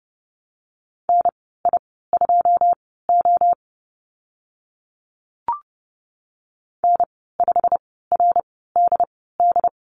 Morse code, heard as a keyed tone, about 30 wpm, spaced as for 17 wpm, 700 Hz. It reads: NI2O N5RDD
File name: Lesson 15g - Number 5 - Call Signs 30wpm